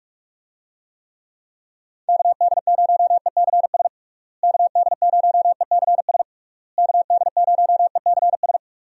Morse code, heard as a keyed tone, about 46 wpm, 700 Hz. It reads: KD0ECS KD0ECS KD0ECS
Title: USA Call Signs - 6 Characters - Repeated 3x - 45wpm